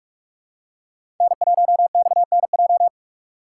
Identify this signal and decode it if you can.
Morse code, heard as a keyed tone, about 44 wpm, 700 Hz. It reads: N1XNJ
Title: USA Call Signs - 6x Character Spacing - 45wpm